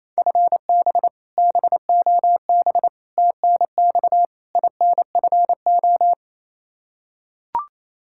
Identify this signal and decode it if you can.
Morse code, heard as a keyed tone, about 28 wpm, 700 Hz. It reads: FB BOB TNX INFO
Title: K1USN SST QSO Practice 28wpm